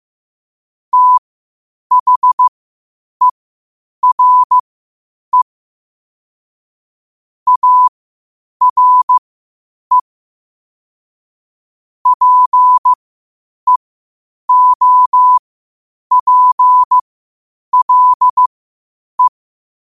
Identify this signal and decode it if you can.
Morse code, heard as a keyed tone, about 15 wpm, spaced as for 8 wpm, 1000 Hz. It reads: THERE ARE PEOPLE
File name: 8wpm